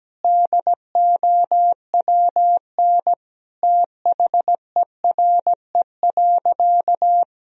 Morse code, heard as a keyed tone, about 17 wpm, 700 Hz. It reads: DOWN THERE.
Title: Sentences from Top 1000 Words - Review Part 1 - 17wpm